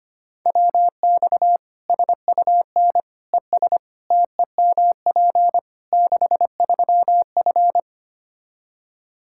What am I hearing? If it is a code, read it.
Morse code, heard as a keyed tone, about 25 wpm, 700 Hz. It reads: WX SUN ES TEMP 63F
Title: Practice QSO - Exchange 2 - Info - 25wpm